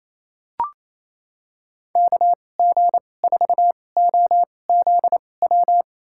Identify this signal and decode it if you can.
Morse code, heard as a keyed tone, about 28 wpm, spaced as for 20 wpm, 700 Hz. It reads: KG4OZW
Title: USA Call Signs - 6 Characters - 2x Character Spacing - 28wpm